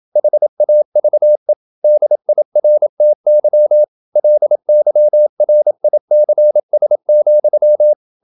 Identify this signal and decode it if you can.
Morse code, heard as a keyed tone, about 27 wpm, 600 Hz. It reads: HAVE DIRTY LYRICS,
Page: Never 26wpm | CW med Gnister